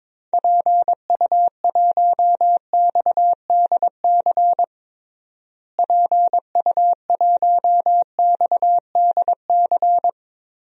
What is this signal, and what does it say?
Morse code, heard as a keyed tone, about 22 wpm, 700 Hz. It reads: PU1XDC PU1XDC
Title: DXCC-100 Call Signs - 6 Characters - Repeated 2x - 22wpm